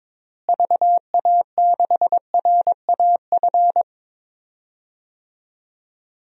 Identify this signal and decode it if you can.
Morse code, heard as a keyed tone, about 22 wpm, 700 Hz. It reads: VA6RAF
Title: DXCC-100 Call Signs - 6 Characters - Repeated 2x - 22wpm